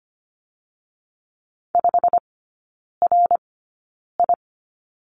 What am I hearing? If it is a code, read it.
Morse code, heard as a keyed tone, about 25 wpm, spaced as for 9 wpm, 690 Hz. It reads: 5RI
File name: Lesson 15e - Number 5 - Three Characters 25wpm